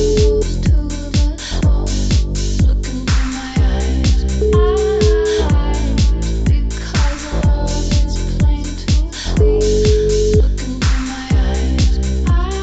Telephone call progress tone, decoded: ringback tone

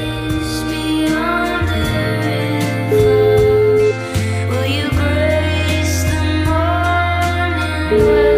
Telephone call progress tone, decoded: ringback tone

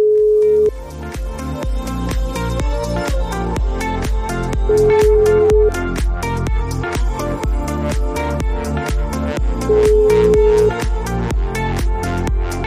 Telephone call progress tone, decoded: ringback tone

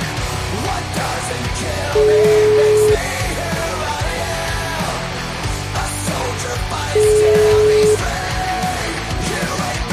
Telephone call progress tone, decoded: ringback tone

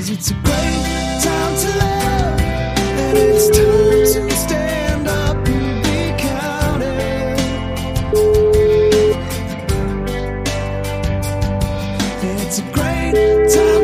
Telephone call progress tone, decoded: ringback tone

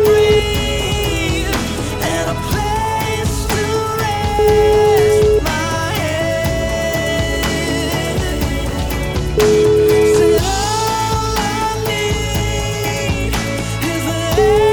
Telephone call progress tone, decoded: ringback tone

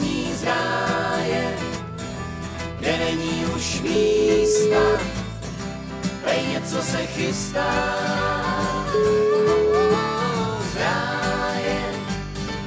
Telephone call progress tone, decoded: ringback tone

